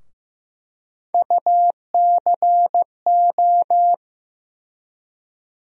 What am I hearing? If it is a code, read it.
Morse code, heard as a keyed tone, about 15 wpm, 700 Hz. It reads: UCO
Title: USA Call Sign Postfixes - 3 Letters 15wpm